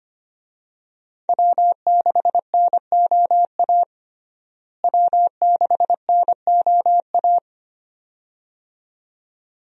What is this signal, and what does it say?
Morse code, heard as a keyed tone, about 25 wpm, 700 Hz. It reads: W6NOA W6NOA
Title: USA Call Signs - 5 Characters - Repeated 2x - 25wpm